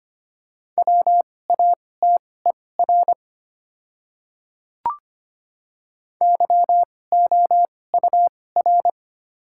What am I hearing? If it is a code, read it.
Morse code, heard as a keyed tone, about 25 wpm, spaced as for 18 wpm, 700 Hz. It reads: WATER YOUR
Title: 2x Character Spacing - Top 300 Words - 25wpm